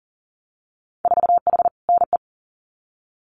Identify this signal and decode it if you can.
Morse code, heard as a keyed tone, about 39 wpm, 700 Hz. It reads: 4H NE
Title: Winter Field Day Exchanges Encore 40wpm